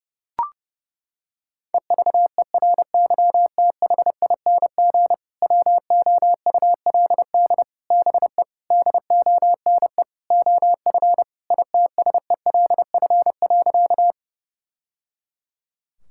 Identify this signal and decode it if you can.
Morse code, heard as a keyed tone, about 32 wpm, 700 Hz. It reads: EVERYTHING WOULD BE DONE OF ITSELF.
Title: Sentences from Top 1000 Words - Review Part 5 - 30wpm